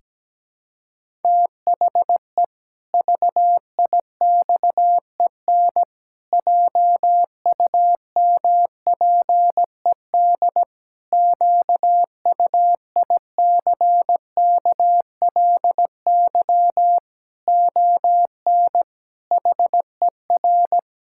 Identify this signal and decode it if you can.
Morse code, heard as a keyed tone, about 17 wpm, 700 Hz. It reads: THE VIXEN JUMPED QUICKLY ON HER
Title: Pangram Challenge v2 - 17wpm